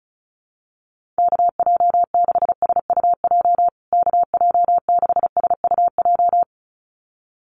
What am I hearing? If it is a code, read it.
Morse code, heard as a keyed tone, about 34 wpm, 700 Hz. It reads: KJ6SUJ KJ6SUJ
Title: USA Call Signs Repeated 2x 35wpm